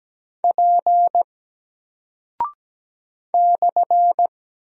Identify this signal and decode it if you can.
Morse code, heard as a keyed tone, about 17 wpm, 700 Hz. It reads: P /
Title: Single Character (Letter:Number:Punctuation:Prosign) 17wpm